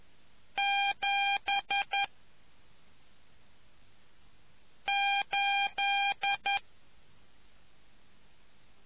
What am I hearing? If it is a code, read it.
Morse code, heard as a keyed tone, about 11 wpm, spaced as for 3 wpm, 800 Hz. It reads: 78